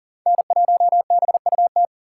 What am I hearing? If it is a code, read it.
Morse code, heard as a keyed tone, about 40 wpm, 690 Hz. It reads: N1BUT
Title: USA Call Signs - 4x Character Spacing - 40wpm